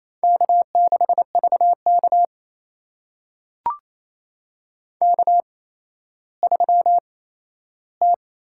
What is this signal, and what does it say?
Morse code, heard as a keyed tone, about 28 wpm, spaced as for 14 wpm, 700 Hz. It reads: K6VK K3T
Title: USA Call Signs - 4 Characters - 8x Character Spacing - 28wpm